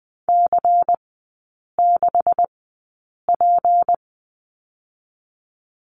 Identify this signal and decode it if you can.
Morse code, heard as a keyed tone, about 20 wpm, spaced as for 8 wpm, 700 Hz. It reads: C6P
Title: 152 Lesson 25e - Letter p - Three Characters 20wpm